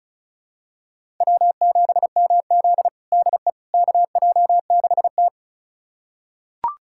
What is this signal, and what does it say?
Morse code, heard as a keyed tone, about 36 wpm, 700 Hz. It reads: W7MZ DE KJ6T
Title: USA Callsign DE Callsign - 4 Characters - 35wpm